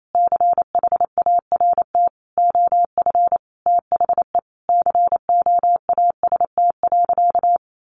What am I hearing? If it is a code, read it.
Morse code, heard as a keyed tone, about 28 wpm, 700 Hz. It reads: CHART OF THE COAST.
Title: Sentences from Top 1300 Words 28wpm